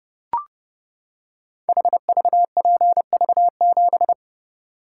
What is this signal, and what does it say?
Morse code, heard as a keyed tone, about 32 wpm, 700 Hz. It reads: HVPV7
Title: Code Groups - 5 Letter:Numbers - 30wpm